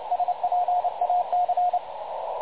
Morse code, heard as a keyed tone, about 29 wpm, 720 Hz, with radio noise. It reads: SPAC
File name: dlazaawansowanych / nagraniazpasma / websdr recording 3565.0kHz